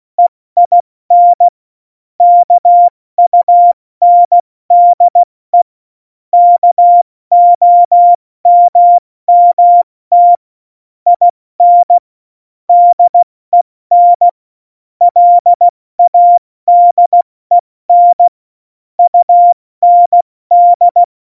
Morse code, heard as a keyed tone, about 16 wpm, 700 Hz. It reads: EIN KUNDE KOMMT IN DEN LADEN UND
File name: Morsekurs / Witze / Witz 15. Januar 2019, 70 F80